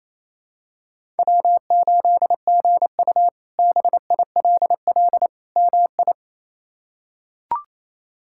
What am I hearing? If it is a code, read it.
Morse code, heard as a keyed tone, about 28 wpm, 700 Hz. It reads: W8GU BILL MI
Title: K1USN SST QSO Practice - 28wpm